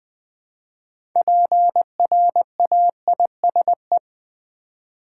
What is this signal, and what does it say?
Morse code, heard as a keyed tone, about 20 wpm, 700 Hz. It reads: PRAISE
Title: 6 Letter Words - First 3 Letters Given - 20wpm